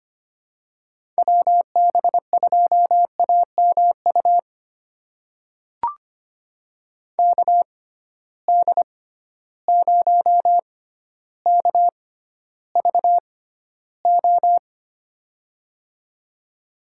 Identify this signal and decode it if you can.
Morse code, heard as a keyed tone, about 25 wpm, spaced as for 13 wpm, 700 Hz. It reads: WB2AMU KD0KVO
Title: USA Call Signs - 6 Characters - 6x Character Spacing - 25wpm